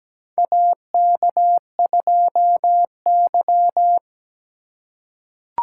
Morse code, heard as a keyed tone, about 17 wpm, 700 Hz. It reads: AK2Y
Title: USA Call Signs - 4 Characters - 4x Character Spacing - 17wpm